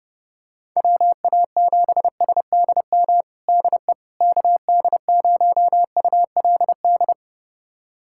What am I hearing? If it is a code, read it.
Morse code, heard as a keyed tone, about 29 wpm, 700 Hz. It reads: WA7SDM DE KD0ULD
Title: USA Callsign DE Callsign - 6 Characters - 30wpm